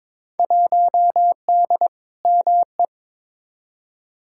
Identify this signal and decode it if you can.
Morse code, heard as a keyed tone, about 22 wpm, 700 Hz. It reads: 1D ME
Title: Field Day Exchanges - 22wpm